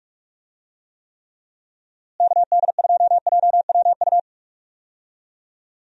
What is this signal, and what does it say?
Morse code, heard as a keyed tone, about 46 wpm, 700 Hz. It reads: KD2JWU
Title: USA Call Signs - 6 Characters - Repeated 3x - 45wpm